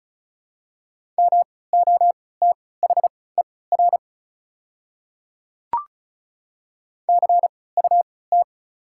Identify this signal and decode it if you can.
Morse code, heard as a keyed tone, about 36 wpm, spaced as for 20 wpm, 700 Hz. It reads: MOTHER CUT
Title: 3x Character Spacing - Top 300 Words - 35wpm